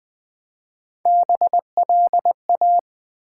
Morse code, heard as a keyed tone, about 20 wpm, 700 Hz. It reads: BLA
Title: Build A Word - 1 Letter At A Time - 3 to 5-Letter Words - 20wpm